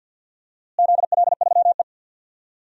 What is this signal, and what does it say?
Morse code, heard as a keyed tone, about 52 wpm, 700 Hz. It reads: ZL3E
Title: DXCC-100 Call Signs - 4 Characters - Repeated 3x - 50wpm